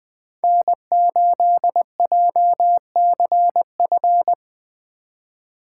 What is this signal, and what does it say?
Morse code, heard as a keyed tone, about 21 wpm, 700 Hz. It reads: N8JCF